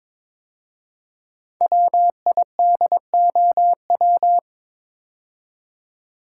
Morse code, heard as a keyed tone, about 22 wpm, 700 Hz. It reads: WIDOW
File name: 5 Letter Words - First 3 Letters Given - 22wpm